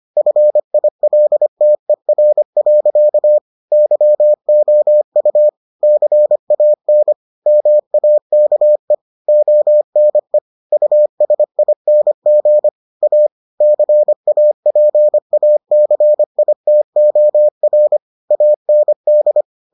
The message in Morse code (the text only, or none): FILTER. YOU CAN MAKE ONE USING A CAPACITOR AND